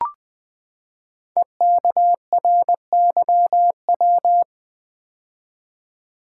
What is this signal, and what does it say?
Morse code, heard as a keyed tone, about 21 wpm, 700 Hz. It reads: EKRYW
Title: Code Groups - 5 Letters - 20wpm